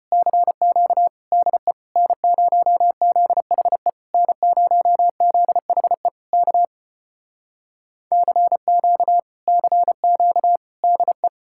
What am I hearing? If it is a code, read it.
Morse code, heard as a keyed tone, about 33 wpm, 700 Hz. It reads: CQ DE N0ZHE N0ZHE K CQ CQ DE